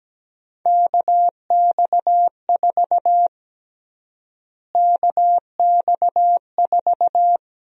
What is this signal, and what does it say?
Morse code, heard as a keyed tone, about 17 wpm, 700 Hz. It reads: KX4 KX4
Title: USA Call Sign Prefixes - Repeated 2x - 17wpm